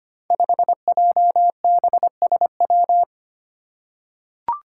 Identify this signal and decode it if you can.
Morse code, heard as a keyed tone, about 25 wpm, 700 Hz. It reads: 5JBSW
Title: Five Challenging Letter-Number Combinations 25wpm